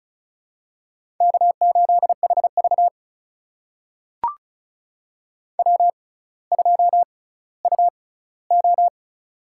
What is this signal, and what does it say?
Morse code, heard as a keyed tone, about 36 wpm, spaced as for 18 wpm, 700 Hz. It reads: K8HV W2UO
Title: USA Call Signs - 4 Characters - 6x Character Spacing - 35wpm